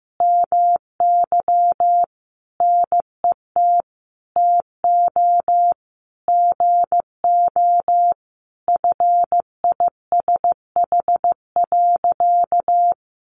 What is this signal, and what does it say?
Morse code, heard as a keyed tone, about 15 wpm, 700 Hz. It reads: MY NET TO GO FISH.
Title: Sentences from Top 1400 Words 15wpm